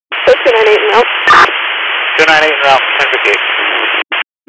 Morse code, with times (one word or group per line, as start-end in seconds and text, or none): none